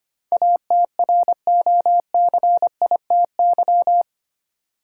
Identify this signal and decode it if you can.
Morse code, heard as a keyed tone, about 25 wpm, 700 Hz. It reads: ATROCITY
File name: Eight Letter Words - 25wpm